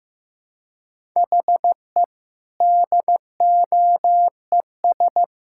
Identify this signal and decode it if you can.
Morse code, heard as a keyed tone, about 15 wpm, 700 Hz. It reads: HE DOES
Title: Sentences from Top 1000 Words - Review Part 5 - 15wpm